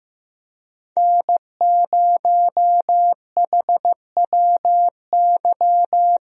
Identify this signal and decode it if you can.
Morse code, heard as a keyed tone, about 15 wpm, 700 Hz. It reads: N0HWY
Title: USA Call Signs - 5 Characters - Repeated 3x - 15wpm